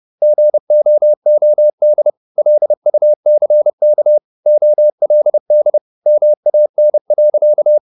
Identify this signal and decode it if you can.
Morse code, heard as a keyed tone, about 30 wpm, 600 Hz. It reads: GOOD LUCK OLD MAN.
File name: the old man and the sea / the old man and the sea 1 30wpm